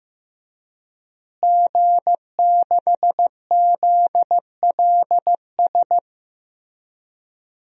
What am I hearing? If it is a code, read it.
Morse code, heard as a keyed tone, about 15 wpm, 700 Hz. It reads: G6ZLS
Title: DXCC-100 Call Signs - 5 Characters - Repeated 3x - 15wpm